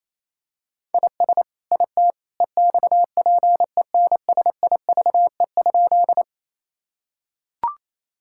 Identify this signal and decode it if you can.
Morse code, heard as a keyed tone, about 27 wpm, 700 Hz. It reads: IS IT EXPENSIVE?
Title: Sentences from Top 1000 Words - Review Part 6 - 28wpm